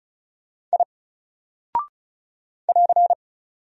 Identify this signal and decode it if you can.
Morse code, heard as a keyed tone, about 36 wpm, 700 Hz. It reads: I <AR>
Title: Single Character (Letter:Number:Punctuation:Prosign) 35wpm